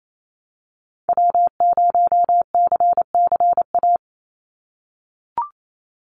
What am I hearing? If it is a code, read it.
Morse code, heard as a keyed tone, about 28 wpm, 700 Hz. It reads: W0CCA
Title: NA SOTA Activator Callsigns 28wpm